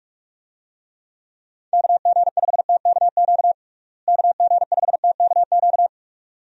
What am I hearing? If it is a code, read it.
Morse code, heard as a keyed tone, about 48 wpm, 700 Hz. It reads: KG5TKQ KG5TKQ
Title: USA Call Signs - 6 Characters - Repeated 2x - 45wpm